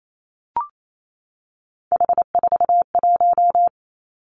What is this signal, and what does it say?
Morse code, heard as a keyed tone, about 28 wpm, 690 Hz. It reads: H41
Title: Three Challenging Letter-Number Combinations 28wpm